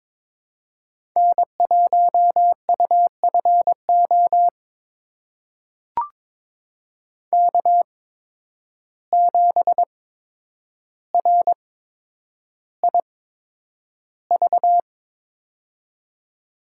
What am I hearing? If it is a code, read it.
Morse code, heard as a keyed tone, about 22 wpm, spaced as for 9 wpm, 700 Hz. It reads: N1UFO K7RIV
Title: USA Call Signs - 8x Character Spacing - 22wpm